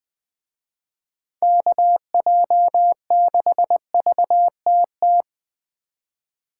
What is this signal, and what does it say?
Morse code, heard as a keyed tone, about 24 wpm, 700 Hz. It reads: KJ6VTT
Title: Speed Racing - USA 2x3 Call Signs 20wpm